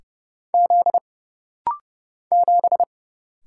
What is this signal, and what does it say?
Morse code, heard as a keyed tone, about 30 wpm, 700 Hz. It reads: Z 7
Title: Single Letter-Number Alt - Rapid-Fire 30wpm